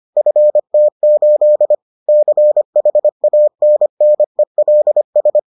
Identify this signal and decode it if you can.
Morse code, heard as a keyed tone, about 25 wpm, 600 Hz. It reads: FT8 CHANNELS